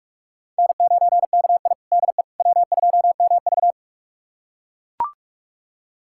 Morse code, heard as a keyed tone, about 44 wpm, 690 Hz. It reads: N9KI DE W2MV
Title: USA Callsign DE Callsign - 4 Characters - 45wpm